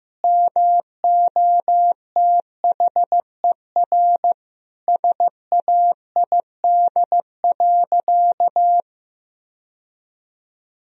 Morse code, heard as a keyed tone, about 15 wpm, 700 Hz. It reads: MOTHER SAID.
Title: Sentences from Top 1000 Words - Review Part 4 - 15wpm